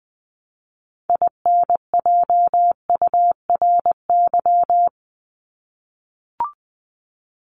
Morse code, heard as a keyed tone, about 20 wpm, 700 Hz. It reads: INJURY
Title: Top 1751-2000 Words 20wpm